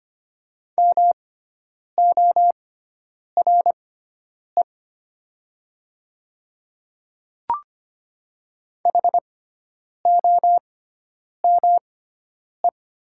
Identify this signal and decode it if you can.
Morse code, heard as a keyed tone, about 25 wpm, spaced as for 9 wpm, 700 Hz. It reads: MORE HOME
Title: 6x Character Spacing - Top 300 Words - 25wpm